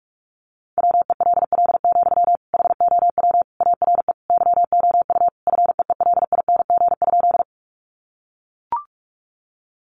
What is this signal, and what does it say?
Morse code, heard as a keyed tone, about 44 wpm, 700 Hz. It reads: WELL, HOW ARE YOU FEELING?
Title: Sentences from Top 1200 Words 45wpm